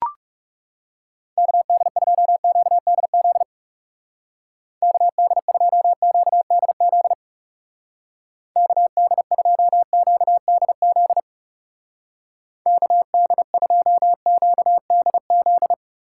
Morse code, heard as a keyed tone, about 39 wpm, 700 Hz. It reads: KD2QDZ KD2QDZ KD2QDZ KD2QDZ